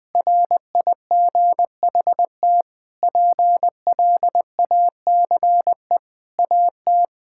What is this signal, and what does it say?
Morse code, heard as a keyed tone, about 20 wpm, 700 Hz. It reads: RIGHT PLACE AT
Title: Sentences with 9-Word Phrases - 20wpm